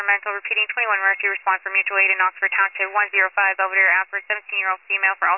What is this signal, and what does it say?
Emergency Alert System (SAME) tones, heard as harsh buzzing bursts, with speech. It is heard from 1.0 to 4.2 s.